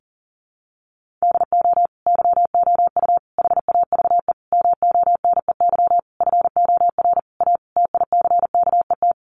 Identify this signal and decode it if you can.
Morse code, heard as a keyed tone, about 37 wpm, 700 Hz. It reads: DO YOU HAVE MONEY FOR A TICKET
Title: Sentences from Top 1400 Words 40wpm